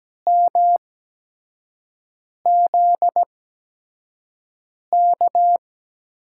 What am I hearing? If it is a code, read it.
Morse code, heard as a keyed tone, about 17 wpm, spaced as for 5 wpm, 700 Hz. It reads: MZK
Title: USA Call Signs - 5 Characters - 8x Character Spacing - 17wpm